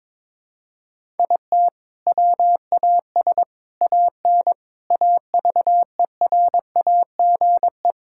Morse code, heard as a keyed tone, about 22 wpm, 700 Hz. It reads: IT WAS AN AVERAGE
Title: Sentences from Top 1000 Words - Review Part 6 - 22wpm